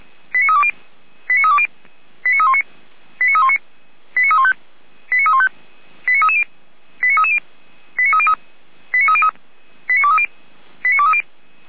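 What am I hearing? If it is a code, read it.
Five-tone selective call (ZVEI), heard as quick runs of five tones, tones of about 70 ms. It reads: 89320 89320 89319 89319 89315 89315 89330 89330 89303 89303 89230 89230
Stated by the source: ZVEI five-tone selective call received on an SDR.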